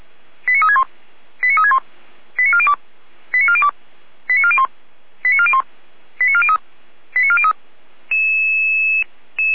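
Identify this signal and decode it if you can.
Five-tone selective call (ZVEI), heard as quick runs of five tones, tones of about 70 ms. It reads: 89361 89361 89402 89402 89401 89401 89403 89403
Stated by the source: ZVEI five-tone selective call received on SDR.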